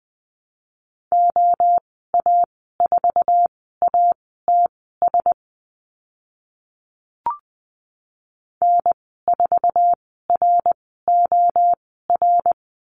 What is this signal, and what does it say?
Morse code, heard as a keyed tone, about 20 wpm, spaced as for 14 wpm, 700 Hz. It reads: OA4ATS N4ROR
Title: 082 Lesson 14g - Number 2 - Call Signs 20wpm